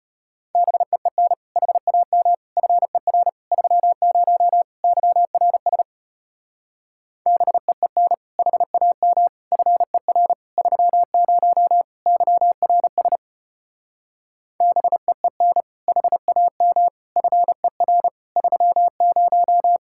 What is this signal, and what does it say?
Morse code, heard as a keyed tone, about 33 wpm, 700 Hz. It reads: BEEN HAM FER 30 YRS BEEN HAM FER 30 YRS BEEN HAM FER 30